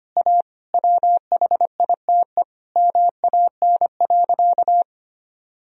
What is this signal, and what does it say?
Morse code, heard as a keyed tone, about 25 wpm, 700 Hz. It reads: A WHITE MAN.
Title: Sentences from Top 1000 Words - Review Part 1 - 25wpm